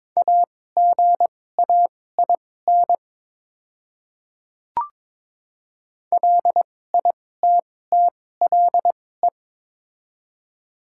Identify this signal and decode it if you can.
Morse code, heard as a keyed tone, about 22 wpm, spaced as for 16 wpm, 700 Hz. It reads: AGAIN LITTLE
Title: 2x Character Spacing - Top 200 Words - 22wpm